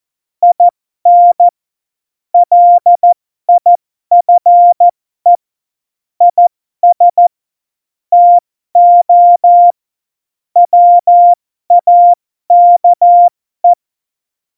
Morse code, heard as a keyed tone, about 14 wpm, spaced as for 12 wpm, 700 Hz. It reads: IN LIFE IS TO WAKE
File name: Morsekurs / Witze